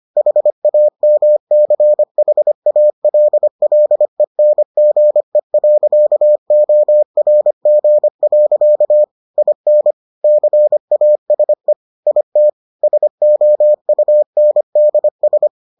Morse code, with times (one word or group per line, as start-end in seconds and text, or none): 0.2-9.0 HAMCHALLENGE.ORG.
9.4-9.9 IN
10.2-11.7 CASE
12.1-12.5 IT
12.8-15.5 SOUNDS